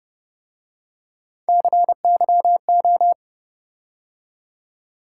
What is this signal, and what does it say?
Morse code, heard as a keyed tone, about 32 wpm, 700 Hz. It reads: CYO